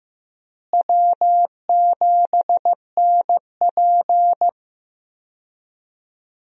Morse code, heard as a keyed tone, about 15 wpm, 700 Hz. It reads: W7NP
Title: USA Call Signs - 4 Characters - Repeated 3x - 15wpm